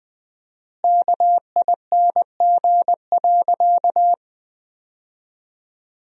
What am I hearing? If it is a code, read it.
Morse code, heard as a keyed tone, about 20 wpm, 700 Hz. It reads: KING.
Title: 4x Word-Spacing - Sentences from Top 400 Words - 20wpm